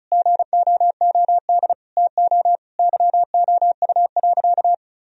Morse code, heard as a keyed tone, about 36 wpm, 700 Hz. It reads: GOOD TO YOU.
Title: Sentences from Top 1000 Words - Review Part 1 - 35wpm